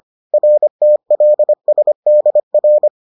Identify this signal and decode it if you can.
Morse code, heard as a keyed tone, about 25 wpm, 600 Hz. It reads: RTLSDR